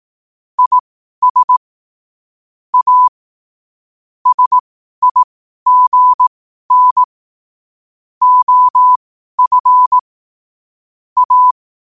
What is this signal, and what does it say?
Morse code, heard as a keyed tone, about 18 wpm, spaced as for 12 wpm, 1000 Hz. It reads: IS A SIGN OF A
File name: qotdfiles / 12wpm